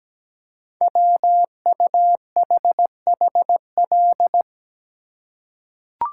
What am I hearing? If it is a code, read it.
Morse code, heard as a keyed tone, about 17 wpm, 700 Hz. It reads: WUHHL
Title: Code Groups - 5 Letters - 17wpm